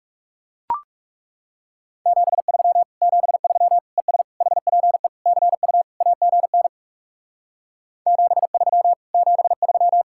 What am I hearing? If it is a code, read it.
Morse code, heard as a keyed tone, about 37 wpm, 700 Hz. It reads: 73 73 ES HPE CU AGN 73 73
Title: Speed-Racing Common QSO Phrases Encore - 30wpm